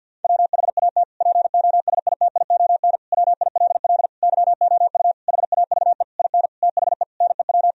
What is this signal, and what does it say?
Morse code, heard as a keyed tone, about 50 wpm, 700 Hz. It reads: WHAT POSITION WILL YOU HAVE IN THE NEW
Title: Sentences from Top 1000 Words - Review Part 3 - 50wpm